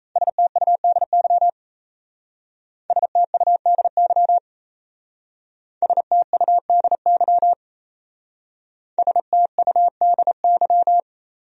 Morse code, heard as a keyed tone, about 35 wpm, 700 Hz. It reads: STUDY STUDY STUDY STUDY
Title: Speed-Racing Top 500 Words 28wpm